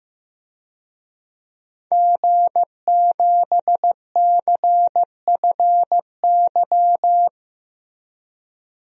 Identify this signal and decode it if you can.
Morse code, heard as a keyed tone, about 15 wpm, 700 Hz. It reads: G7CFY
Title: DXCC-100 Call Signs - 5 Characters - 15wpm